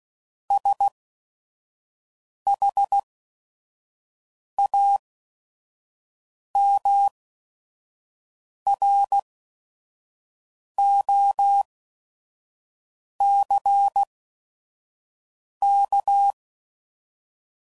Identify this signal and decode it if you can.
Morse code, heard as a keyed tone, about 16 wpm, spaced as for 5 wpm, 800 Hz. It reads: SHAMROCK
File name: CW Code Groups @5WPM / fromSM's / Cities